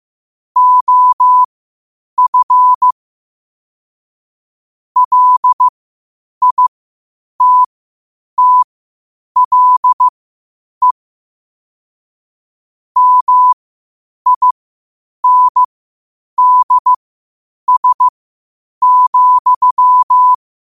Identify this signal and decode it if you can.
Morse code, heard as a keyed tone, about 15 wpm, spaced as for 8 wpm, 1000 Hz. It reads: OF LITTLE MINDS,
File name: qotdfiles / 8wpm